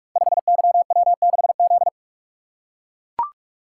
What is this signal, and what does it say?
Morse code, heard as a keyed tone, about 44 wpm, 700 Hz. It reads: 5YW6Z
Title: Code Groups - 5 Letter:Numbers - 45wpm